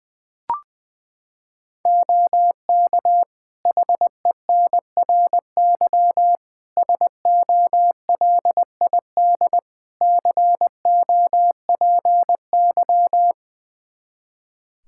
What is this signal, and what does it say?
Morse code, heard as a keyed tone, about 20 wpm, 700 Hz. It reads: OK HENRY SOLID COPY
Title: Practice QSO - Exchange 3 - More Info - 20wpm